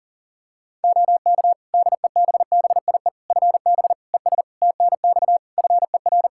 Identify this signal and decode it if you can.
Morse code, heard as a keyed tone, about 40 wpm, 700 Hz. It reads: OK DEBBIE FB ES TNX FER
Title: Practice QSO - Exchange 2 - Info - 40wpm